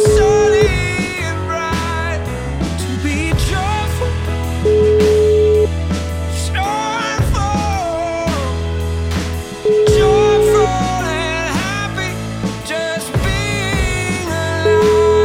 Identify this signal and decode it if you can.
Ringback tone (Europe, 425 Hz), with noise.